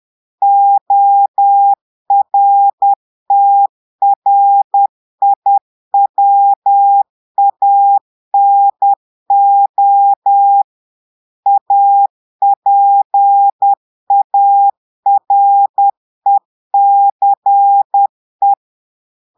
Morse code, heard as a keyed tone, about 10 wpm, 800 Hz. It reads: ORTRIWANO APARECE
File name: Portugais - texte en clair 1 - vitesse - 10 mots minute